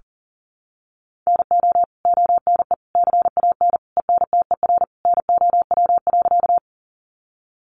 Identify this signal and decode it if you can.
Morse code, heard as a keyed tone, about 37 wpm, 700 Hz. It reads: NO ONE CAN ENTER NOW.
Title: Sentences from Top 1300 Words 40wpm